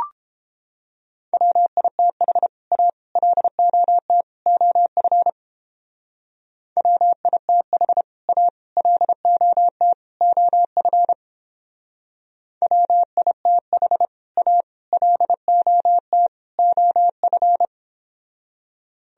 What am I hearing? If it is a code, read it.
Morse code, heard as a keyed tone, about 30 wpm, 700 Hz. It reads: WITH A LOT OF WITH A LOT OF WITH A LOT OF